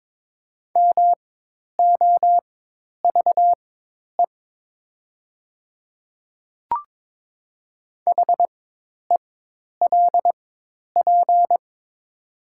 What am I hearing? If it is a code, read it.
Morse code, heard as a keyed tone, about 22 wpm, spaced as for 10 wpm, 700 Hz. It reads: MOVE HELP
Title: 4x Character Spacing - Top 300 Words - 22wpm